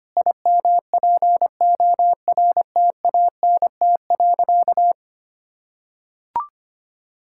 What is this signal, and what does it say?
Morse code, heard as a keyed tone, about 25 wpm, 700 Hz. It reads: IMPORTANT.
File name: practice / 25wpm / Sentences with 9-Word Phrases - 25wpm